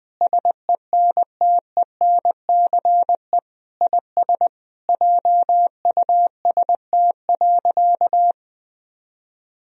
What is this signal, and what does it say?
Morse code, heard as a keyed tone, about 21 wpm, 700 Hz. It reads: SENTENCE IS JUST.
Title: Sentences from Top 1000 Words - Review Part 1 - 20wpm